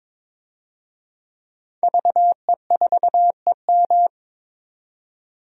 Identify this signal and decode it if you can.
Morse code, heard as a keyed tone, about 22 wpm, 700 Hz. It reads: VE4EM
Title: DXCC-100 Call Signs - 5 Characters - 22wpm